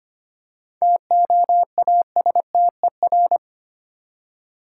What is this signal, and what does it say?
Morse code, heard as a keyed tone, about 25 wpm, 700 Hz. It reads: TOASTER
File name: practice / 25wpm / Easy Words with Hints - 25wpm